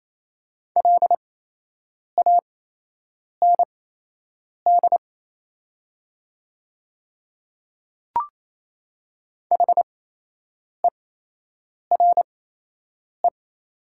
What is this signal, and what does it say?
Morse code, heard as a keyed tone, about 28 wpm, spaced as for 8 wpm, 700 Hz. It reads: LAND HERE
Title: 8x Character Spacing - Top 200 Words - 28wpm